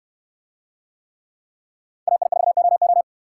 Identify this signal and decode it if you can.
Morse code, heard as a keyed tone, about 69 wpm, 700 Hz. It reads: AI4OO